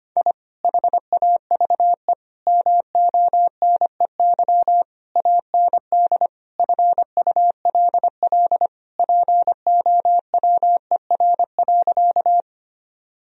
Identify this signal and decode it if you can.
Morse code, heard as a keyed tone, about 25 wpm, 700 Hz. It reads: I HAVE MONEY AND FULL POWER.
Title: Sentences from Top 1000 Words - Review Part 3 - 25wpm